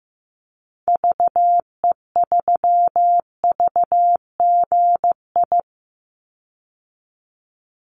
Morse code, heard as a keyed tone, about 15 wpm, 700 Hz. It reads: VE3VGI
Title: Call Signs Repeated 3x - 15wpm